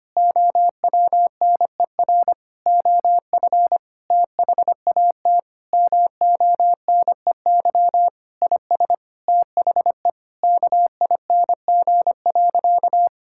Morse code, heard as a keyed tone, about 25 wpm, 700 Hz. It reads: OWNER OF THAT MONEY IS THE KING.
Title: Sentences from Top 1600 Words 25wpm